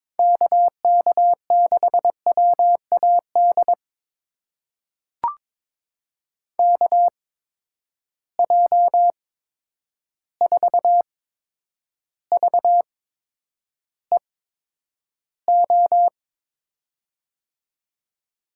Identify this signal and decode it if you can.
Morse code, heard as a keyed tone, about 22 wpm, spaced as for 9 wpm, 700 Hz. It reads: KK6WAD KJ4VEO
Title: USA Call Signs - 6 Characters - 8x Character Spacing - 22wpm